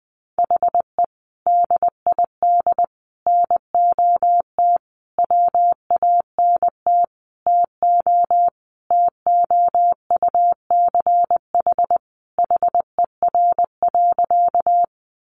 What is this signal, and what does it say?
Morse code, heard as a keyed tone, about 20 wpm, 700 Hz. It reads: HE DID NOT WANT TO TOUCH HER.
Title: Sentences from Top 1200 Words 20wpm